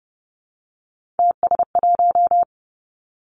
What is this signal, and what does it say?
Morse code, heard as a keyed tone, about 30 wpm, 700 Hz. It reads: TS1
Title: Three Challenging Letter-Number Combinations 30wpm